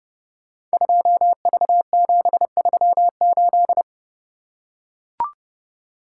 Morse code, heard as a keyed tone, about 30 wpm, 700 Hz. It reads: 2V738
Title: Code Groups - 5 Letter:Numbers - 30wpm